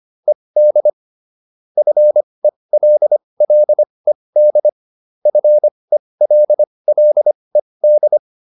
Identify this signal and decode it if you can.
Morse code, heard as a keyed tone, about 25 wpm, spaced as for 20 wpm, 600 Hz. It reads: ED FELLED FELLED